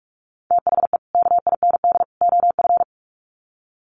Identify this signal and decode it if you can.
Morse code, heard as a keyed tone, about 44 wpm, 700 Hz. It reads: THE KIND OF
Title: Sets of 3 Words 45wpm